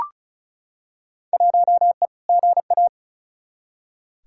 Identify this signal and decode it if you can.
Morse code, heard as a keyed tone, about 36 wpm, 700 Hz. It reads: 1E GA
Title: Field Day Exchanges - 35wpm